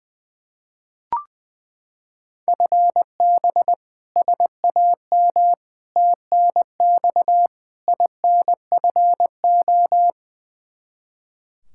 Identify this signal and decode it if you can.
Morse code, heard as a keyed tone, about 20 wpm, 700 Hz. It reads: FB SAM TNX INFO
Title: Practice QSO - Exchange 3 - More Info - 20wpm